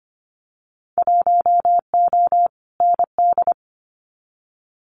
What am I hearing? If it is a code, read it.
Morse code, heard as a keyed tone, about 25 wpm, 700 Hz. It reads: 1O ND
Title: Winter Field Day Exchanges 25wpm